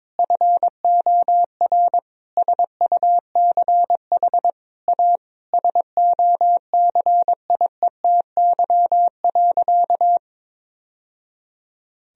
Morse code, heard as a keyed tone, about 22 wpm, 700 Hz. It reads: FOR SUCH A SOCIETY.
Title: Sentences from Top 1000 Words - Review Part 4 - 22wpm